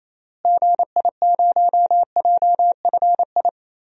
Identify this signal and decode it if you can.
Morse code, heard as a keyed tone, about 28 wpm, 700 Hz. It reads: GI0JFI
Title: DXCC-100 Call Signs - 6 Characters - Repeated 3x - 28wpm